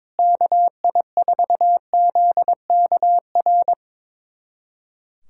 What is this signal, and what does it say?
Morse code, heard as a keyed tone, about 22 wpm, 700 Hz. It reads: KI4ZKR
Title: USA Callsign DE Callsign - 6 Characters - 22wpm